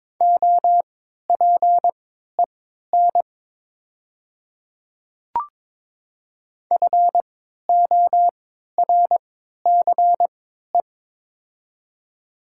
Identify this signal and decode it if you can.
Morse code, heard as a keyed tone, about 22 wpm, spaced as for 12 wpm, 700 Hz. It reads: OPEN FORCE
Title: 3x Character Spacing - Top 500 Words - 22wpm